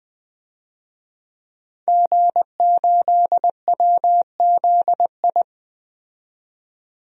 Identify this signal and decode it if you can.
Morse code, heard as a keyed tone, about 21 wpm, 700 Hz. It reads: G8WZI
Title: DXCC-100 Call Signs - 5 Characters - 20wpm